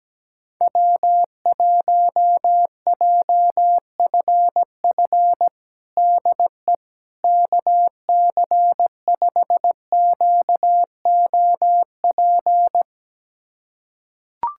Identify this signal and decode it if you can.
Morse code, heard as a keyed tone, about 17 wpm, 700 Hz. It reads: W1JFF DE KC5QOP
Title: USA Callsign DE Callsign 17wpm